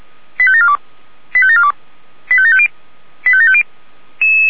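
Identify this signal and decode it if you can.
Five-tone selective call (ZVEI), heard as quick runs of five tones, tones of about 70 ms. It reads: 85742 85742 85750 85750
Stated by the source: ZVEI five-tone selective call received with an SDR, recorded by feuerwehr-orlamuende fire brigade.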